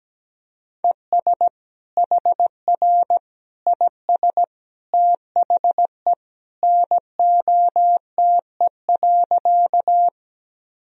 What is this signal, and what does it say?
Morse code, heard as a keyed tone, about 17 wpm, 700 Hz. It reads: ES HR IS THE NOTE.
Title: Sentences from Top 500 Words and Abbreviations 17wpm